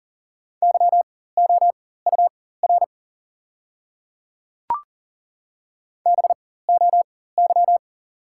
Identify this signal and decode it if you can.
Morse code, heard as a keyed tone, about 40 wpm, spaced as for 19 wpm, 690 Hz. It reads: YOUR BOY
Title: 4x Character Spacing - Top 300 Words - 40wpm